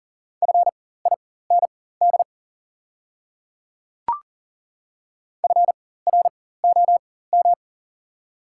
Morse code, heard as a keyed tone, about 37 wpm, spaced as for 18 wpm, 700 Hz. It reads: FIND FROM
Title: 4x Character Spacing - Top 100 Words - 40wpm